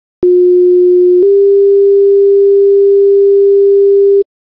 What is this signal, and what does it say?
Two-tone page, heard as a one-second tone, then a three-second tone, about 360 then 400 Hz. It is heard from 0.2 to 4.2 s.